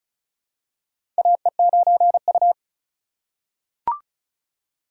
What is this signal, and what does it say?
Morse code, heard as a keyed tone, about 36 wpm, 700 Hz. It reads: AE9U
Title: USA Call Signs - 4 Characters - 2x Character Spacing - 35wpm